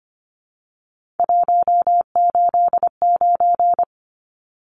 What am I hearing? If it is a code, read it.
Morse code, heard as a keyed tone, about 25 wpm, 700 Hz. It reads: 189
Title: Numbers - 3-Digit 25wpm